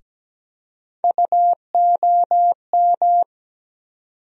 Morse code, heard as a keyed tone, about 17 wpm, 700 Hz. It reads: UOM
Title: USA Call Sign Postfixes - 3 Letters 17wpm